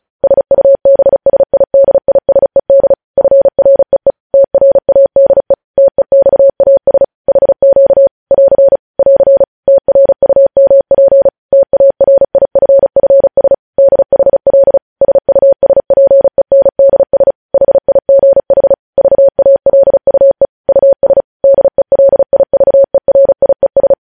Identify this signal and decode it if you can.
Morse code, heard as a keyed tone, about 36 wpm, 550 Hz. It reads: SUBSIDISED FREE TRADE TEXAS HQ <AR> <AR> TRUMP TARIFFS DHL SUSPENDS HIGH VALUE US DELIVERIES
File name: podcast / News headlines-1x-35x35